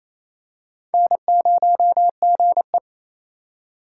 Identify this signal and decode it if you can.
Morse code, heard as a keyed tone, about 28 wpm, 700 Hz. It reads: N0GE
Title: USA Call Signs - 4 Characters - 2x Character Spacing - 28wpm